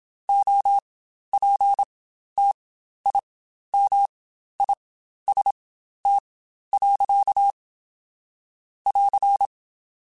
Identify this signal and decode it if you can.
Morse code, heard as a keyed tone, about 27 wpm, spaced as for 12 wpm, 800 Hz. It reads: OPTIMIST. <AR>
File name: Quotes Spaced / Quotes Spaced-mp3s / Optimist sp @25